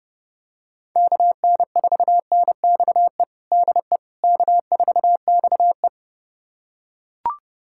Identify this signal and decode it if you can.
Morse code, heard as a keyed tone, about 32 wpm, 700 Hz. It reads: KN4NXE DE K4XE
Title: USA Callsign DE Callsign 30wpm